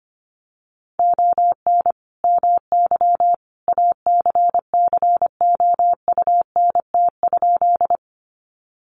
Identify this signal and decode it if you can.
Morse code, heard as a keyed tone, about 25 wpm, 700 Hz. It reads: ON MY ACCOUNT?
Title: Sentences from Top 900 Words 25wpm